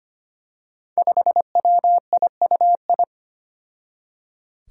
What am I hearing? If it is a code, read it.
Morse code, heard as a keyed tone, about 25 wpm, 700 Hz. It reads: 5WIUI
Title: Five Challenging Letter-Number Combinations 25wpm